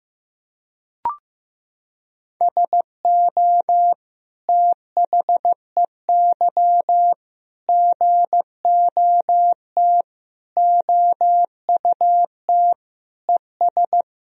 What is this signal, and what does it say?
Morse code, heard as a keyed tone, about 16 wpm, 700 Hz. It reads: SO THEY GOT OUT ES
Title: Sentences from Top 300 Words and Abbreviations 15wpm